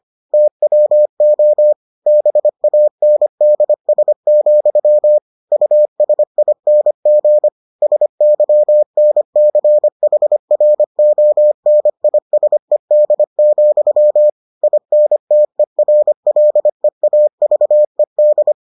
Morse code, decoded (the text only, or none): TWO BANDS, USING SYNCHRONISED, INTERLEAVED